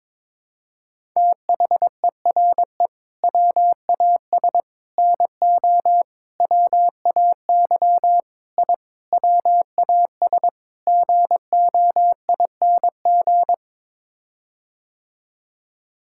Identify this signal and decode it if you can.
Morse code, heard as a keyed tone, about 22 wpm, 700 Hz. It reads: THERE WAS NO WAY I WAS GOING
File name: practice / 22wpm / Sets of 7 Words - 22wpm